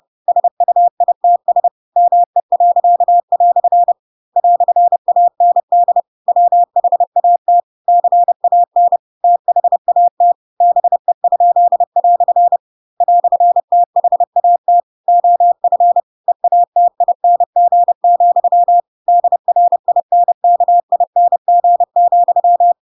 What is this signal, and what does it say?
Morse code, heard as a keyed tone, about 30 wpm, 700 Hz. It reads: SUITS ME." "AND WHAT CAN THAT BE?" "THAT OF EATING, DRINKING,
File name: Pinocchio / Pinocchio ch4 30 30 ik1hge